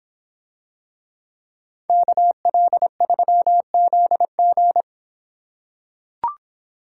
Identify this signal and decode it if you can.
Morse code, heard as a keyed tone, about 26 wpm, 700 Hz. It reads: KL3ZG